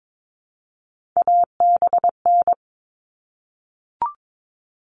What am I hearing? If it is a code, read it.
Morse code, heard as a keyed tone, about 22 wpm, 700 Hz. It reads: ABN
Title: All Cut Numbers - 3-Digit 22wpm